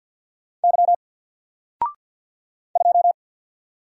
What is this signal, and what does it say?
Morse code, heard as a keyed tone, about 48 wpm, 700 Hz. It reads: Y 2
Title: Single Letter-Number 50wpm